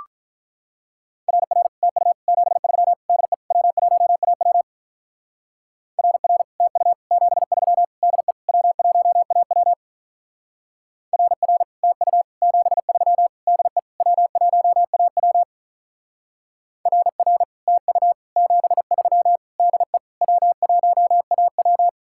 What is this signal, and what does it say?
Morse code, heard as a keyed tone, about 43 wpm, 700 Hz. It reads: RR TU 73 DE W1AW RR TU 73 DE W1AW RR TU 73 DE W1AW RR TU 73 DE W1AW
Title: Speed-Racing Common QSO Phrases Encore - 35wpm